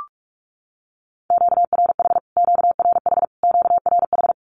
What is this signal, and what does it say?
Morse code, heard as a keyed tone, about 44 wpm, 700 Hz. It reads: QRH QRH QRH
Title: Q-Codes Repeated 3x 45wpm